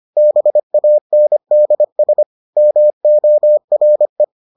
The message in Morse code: BANDS MORE

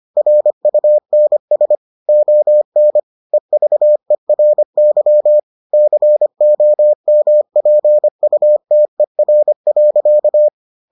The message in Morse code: RUNS ON EVERY COMPUTER.

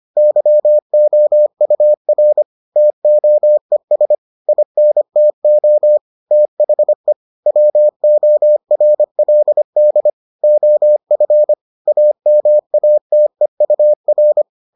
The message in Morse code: YOUR TOES INTO THE WORLD OF AMATEUR

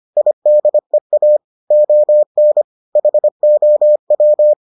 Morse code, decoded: IDEA ON HOW

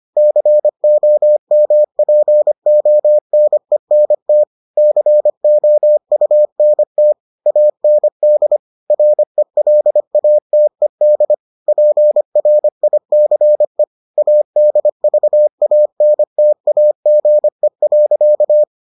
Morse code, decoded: COMPONENT COUNT AND RELATED PRICE ADVANTAGE.

